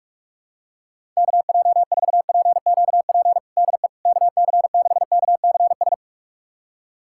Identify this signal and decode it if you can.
Morse code, heard as a keyed tone, about 46 wpm, 700 Hz. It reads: KJ4PQP DE KC6KCS
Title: USA Callsign DE Callsign - 6 Characters - 45wpm